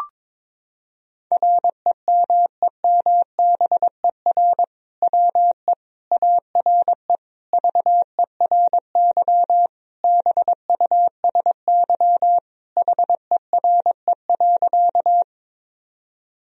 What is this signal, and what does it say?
Morse code, heard as a keyed tone, about 22 wpm, 700 Hz. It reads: REMEMBER WE ARE VERY BUSY HERE.
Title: Sentences from Top 1000 Words - Review Part 3 - 22wpm